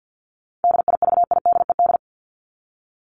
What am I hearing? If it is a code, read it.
Morse code, heard as a keyed tone, about 50 wpm, 700 Hz. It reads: DIVIDED
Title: ED Suffix - Top 100 Words with Base Words 50wpm